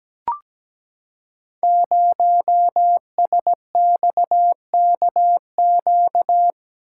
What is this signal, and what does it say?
Morse code, heard as a keyed tone, about 17 wpm, 700 Hz. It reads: 0SXKQ
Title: Code Groups - 5 Letter:Numbers - 17wpm